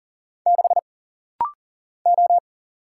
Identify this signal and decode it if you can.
Morse code, heard as a keyed tone, about 40 wpm, 700 Hz. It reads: 6 O
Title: Single Letter-Number Alt - Rapid-Fire 40wpm